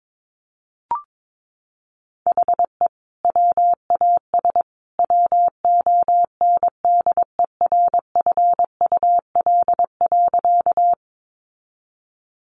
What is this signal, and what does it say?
Morse code, heard as a keyed tone, about 22 wpm, 700 Hz. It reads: HE WAS WONDERFUL.
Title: Sentences from Top 1100 Words 22wpm